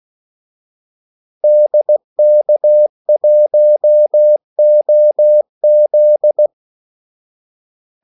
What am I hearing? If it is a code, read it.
Morse code, heard as a keyed tone, about 16 wpm, 600 Hz. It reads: DK1OZ
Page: Kaldesignaler 16 wpm